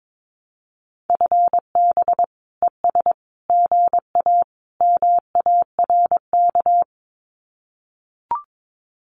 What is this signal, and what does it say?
Morse code, heard as a keyed tone, about 22 wpm, 700 Hz. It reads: FB ES GA MARK
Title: Common QSO Phrases Encore 22wpm